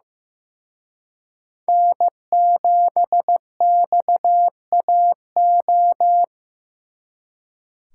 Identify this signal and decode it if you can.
Morse code, heard as a keyed tone, about 18 wpm, 700 Hz. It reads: N7XAO
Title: Speed Racing - USA 1x3 Call Signs 15wpm